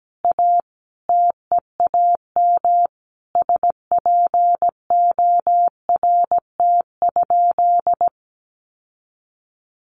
Morse code, heard as a keyed tone, about 17 wpm, 700 Hz. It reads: A TEAM SPORT?
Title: Sentences from Top 1500 Words 17wpm